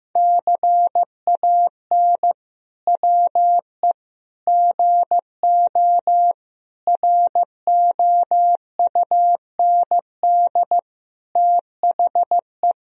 Morse code, heard as a keyed tone, about 15 wpm, 700 Hz. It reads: CAN WE GO ROUND THE
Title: Sentences from Top 1000 Words - Review Part 1 - 15wpm